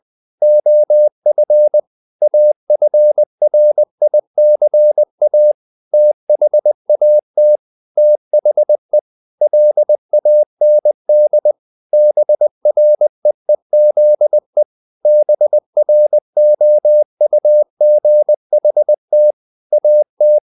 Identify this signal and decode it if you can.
Morse code, heard as a keyed tone, about 20 wpm, 600 Hz. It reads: OF AFRICA THAT THE LAND BREEZE BROUGHT AT
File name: the old man and the sea / the old man and the sea part 1 20wpm